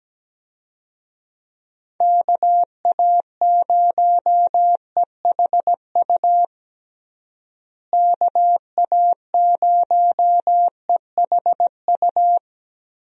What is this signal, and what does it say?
Morse code, heard as a keyed tone, about 17 wpm, 700 Hz. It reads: KA0EHU KA0EHU
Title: USA Call Signs - 6 Characters - Repeated 3x - 17wpm